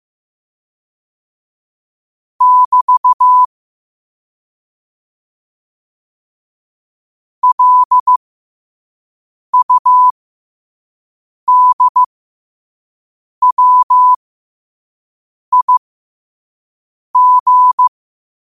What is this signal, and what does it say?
Morse code, heard as a keyed tone, about 15 wpm, spaced as for 5 wpm, 1000 Hz. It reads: <BT> LUDWIG
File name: qotdfiles / 5wpm